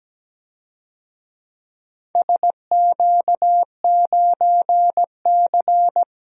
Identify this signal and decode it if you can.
Morse code, heard as a keyed tone, about 17 wpm, 700 Hz. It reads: SQ9C
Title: DXCC-100 Call Signs - 4 Characters - 17wpm